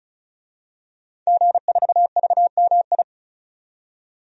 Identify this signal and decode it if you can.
Morse code, heard as a keyed tone, about 34 wpm, 700 Hz. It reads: G4VMI
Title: DXCC-100 Call Signs - 5 Characters - 35wpm